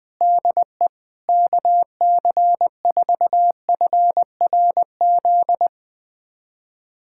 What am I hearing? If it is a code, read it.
Morse code, heard as a keyed tone, about 21 wpm, 700 Hz. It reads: DE KC4FRZ
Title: USA Callsign DE Callsign - 6 Characters - 20wpm